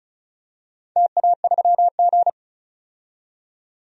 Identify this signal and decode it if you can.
Morse code, heard as a keyed tone, about 36 wpm, 700 Hz. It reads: TA3G
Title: DXCC-100 Call Signs - 4 Characters - 35wpm